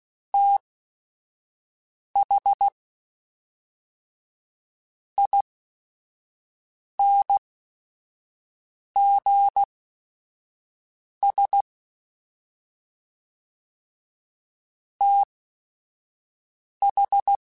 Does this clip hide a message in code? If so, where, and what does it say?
Yes: Morse code from 0.3 to 17.4 s, heard as a keyed tone, about 16 wpm, spaced as for 5 wpm, 800 Hz. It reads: TH INGS TH